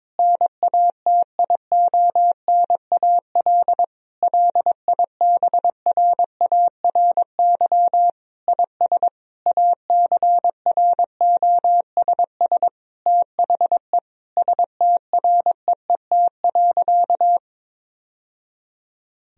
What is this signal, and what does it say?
Morse code, heard as a keyed tone, about 22 wpm, 700 Hz. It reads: NATIONAL LIBRARY IS ACROSS THE STREET.
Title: Sentences from Top 1000 Words - Review Part 4 - 22wpm